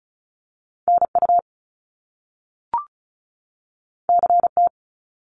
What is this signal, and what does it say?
Morse code, heard as a keyed tone, about 36 wpm, 700 Hz. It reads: NU CT
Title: Top Two Letter Combinations in Words 35wpm